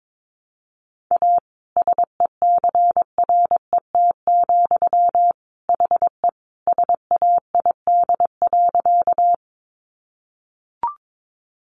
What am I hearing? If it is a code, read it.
Morse code, heard as a keyed tone, about 22 wpm, 700 Hz. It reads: A SECRET, HE SAID.
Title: Sentences from Top 1200 Words 22wpm